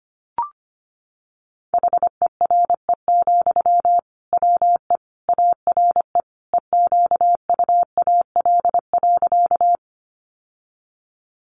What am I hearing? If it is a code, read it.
Morse code, heard as a keyed tone, about 25 wpm, 700 Hz. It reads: HERE, WE ARE EQUAL.
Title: Sentences from Top 1400 Words 25wpm